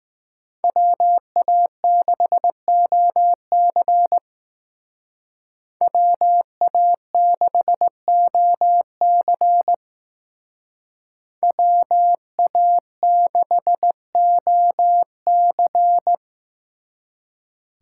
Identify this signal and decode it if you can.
Morse code, heard as a keyed tone, about 18 wpm, 700 Hz. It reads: WA6OC WA6OC WA6OC
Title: Speed Racing - USA 2x2 Call Signs 15wpm